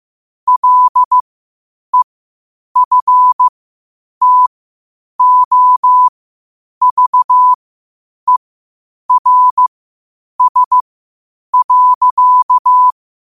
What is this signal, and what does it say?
Morse code, heard as a keyed tone, about 15 wpm, spaced as for 8 wpm, 1000 Hz. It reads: LEFTOVERS.
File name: qotdfiles / 8wpm